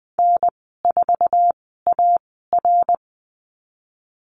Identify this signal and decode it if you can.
Morse code, heard as a keyed tone, about 20 wpm, spaced as for 14 wpm, 700 Hz. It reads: N4AR
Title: 062 Lesson 11g - Letter h - Call Signs 20wpm